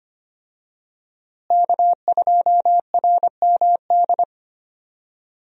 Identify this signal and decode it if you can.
Morse code, heard as a keyed tone, about 25 wpm, 700 Hz. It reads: K2RMD
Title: USA Call Signs - 5 Characters - Repeated 2x - 25wpm